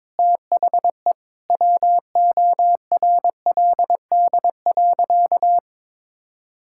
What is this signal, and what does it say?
Morse code, heard as a keyed tone, about 22 wpm, 700 Hz. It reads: THE WORLD.
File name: Sentences with 9-Word Phrases - 22wpm